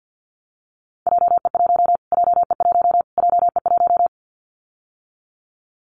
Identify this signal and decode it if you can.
Morse code, heard as a keyed tone, about 50 wpm, 700 Hz. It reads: JE1 JE1 JE1
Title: Callsign Prefixes for Top 25 DXCC - Repeated 3x - 50wpm